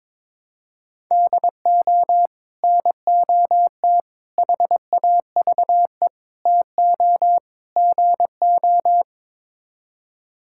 Morse code, heard as a keyed tone, about 22 wpm, 700 Hz. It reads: DO NOT HAVE TO GO
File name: practice / 22wpm / Build A Phrase - 5-Word Phrases - 22wpm